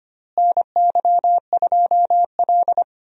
Morse code, heard as a keyed tone, about 25 wpm, 700 Hz. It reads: NY2L
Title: USA Call Signs - 4 Characters - Repeated 3x - 25wpm